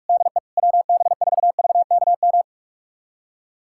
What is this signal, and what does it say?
Morse code, heard as a keyed tone, about 41 wpm, 700 Hz. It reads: DE WB4VKM
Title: USA Callsign DE Callsign - 6 Characters - 45wpm